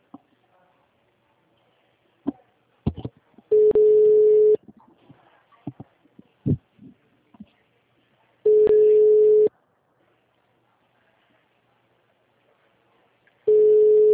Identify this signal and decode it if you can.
Ringback tone (Europe, 425 Hz).